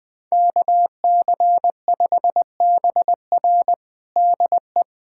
Morse code, heard as a keyed tone, about 20 wpm, 700 Hz. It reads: KC5BR DE
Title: USA Callsign DE Callsign 20wpm